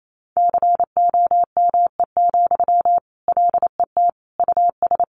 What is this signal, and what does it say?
Morse code, heard as a keyed tone, about 28 wpm, 700 Hz. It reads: COME, LET US
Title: Sentences from Top 1400 Words 28wpm